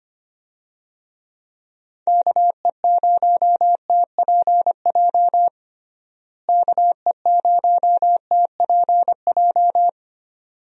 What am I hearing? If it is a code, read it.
Morse code, heard as a keyed tone, about 25 wpm, 700 Hz. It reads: KE0TPJ KE0TPJ
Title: USA Call Signs - 6 Characters - Repeated 2x - 25wpm